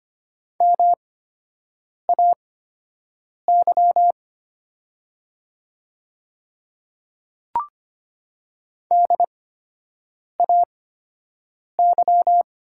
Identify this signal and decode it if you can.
Morse code, heard as a keyed tone, about 25 wpm, spaced as for 7 wpm, 700 Hz. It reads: MAY DAY
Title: 8x Character Spacing - Top 100 Words - 25wpm